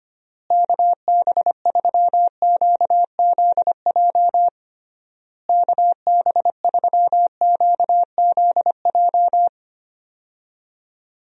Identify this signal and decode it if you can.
Morse code, heard as a keyed tone, about 25 wpm, 700 Hz. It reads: KB3QZJ KB3QZJ
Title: USA Call Signs - 6 Characters - Repeated 3x - 25wpm